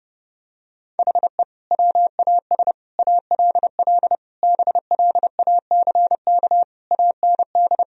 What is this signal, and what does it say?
Morse code, heard as a keyed tone, about 32 wpm, 700 Hz. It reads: HE WAS ALL BLACK AND
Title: Sentences from Top 1000 Words - Review Part 3 - 30wpm